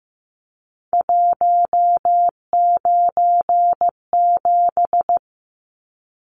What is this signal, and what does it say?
Morse code, heard as a keyed tone, about 15 wpm, 700 Hz. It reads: 197
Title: Numbers - 3-Digit 15wpm